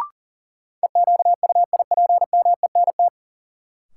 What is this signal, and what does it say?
Morse code, heard as a keyed tone, about 40 wpm, 700 Hz. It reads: EQUIPMENT
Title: Sentences from Top 700 Words - New Word Drill - 40wpm